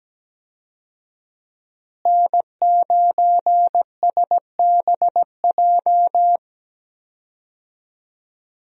Morse code, heard as a keyed tone, about 17 wpm, 700 Hz. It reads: N9SBJ
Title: USA Call Signs - 5 Characters - Repeated 3x - 17wpm